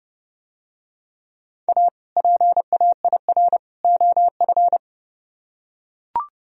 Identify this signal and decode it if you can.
Morse code, heard as a keyed tone, about 30 wpm, 700 Hz. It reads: A PAIR OF